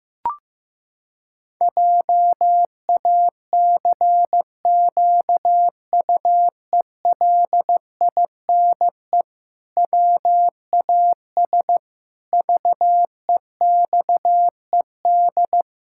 Morse code, heard as a keyed tone, about 15 wpm, 700 Hz. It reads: JACQUELINE WAS VEXED
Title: Pangram Challenge v2 - 15wpm